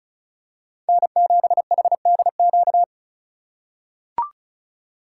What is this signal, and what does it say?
Morse code, heard as a keyed tone, about 34 wpm, 700 Hz. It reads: N7HDQ
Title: USA Call Signs - 5 Characters - 2x Character Spacing - 35wpm